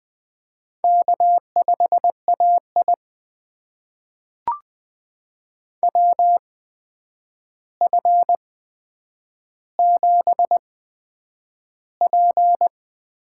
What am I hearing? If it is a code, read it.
Morse code, heard as a keyed tone, about 20 wpm, spaced as for 9 wpm, 700 Hz. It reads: K5AI WF7P
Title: USA Call Signs - 4 Characters - 8x Character Spacing - 20wpm